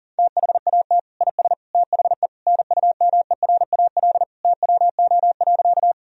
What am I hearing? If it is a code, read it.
Morse code, heard as a keyed tone, about 37 wpm, 700 Hz. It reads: THAT IS THE NUMERAL TWO.
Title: Sentences from Top 1000 Words - Review Part 2 - 40wpm